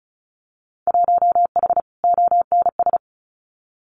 Morse code, heard as a keyed tone, about 36 wpm, 700 Hz. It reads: 1H ONS
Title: Winter Field Day Exchanges 35wpm